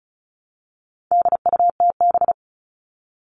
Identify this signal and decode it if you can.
Morse code, heard as a keyed tone, about 34 wpm, 700 Hz. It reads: DUTB
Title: All Cut Numbers - 4-Digit 35wpm